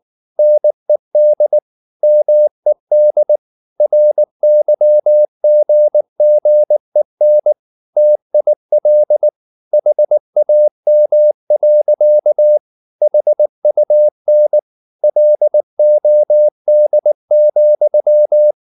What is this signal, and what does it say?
Morse code, heard as a keyed tone, about 19 wpm, 600 Hz. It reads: NED MED RYGGEN TIL HAM. HUN LOD,
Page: Ferie 19 wpm